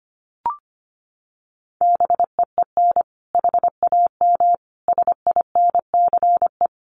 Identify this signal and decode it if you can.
Morse code, heard as a keyed tone, about 25 wpm, 700 Hz. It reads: BEEN HAM SINCE
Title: Common QSO Phrases 25wpm